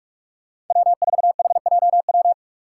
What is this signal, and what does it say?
Morse code, heard as a keyed tone, about 44 wpm, 700 Hz. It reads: W4HJW
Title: Five Challenging Letter-Number Combinations 45wpm